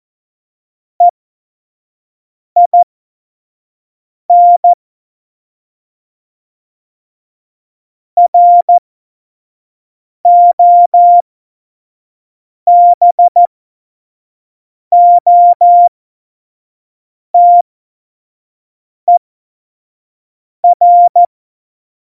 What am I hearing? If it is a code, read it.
Morse code, heard as a keyed tone, about 14 wpm, spaced as for 5 wpm, 700 Hz. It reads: EIN ROBOTER
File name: Morsekurs / Witze / Witz 21. November 2017, 25 F70